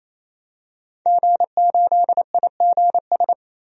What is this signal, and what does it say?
Morse code, heard as a keyed tone, about 28 wpm, 700 Hz. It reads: G8IGS
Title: DXCC-100 Call Signs - 5 Characters - 28wpm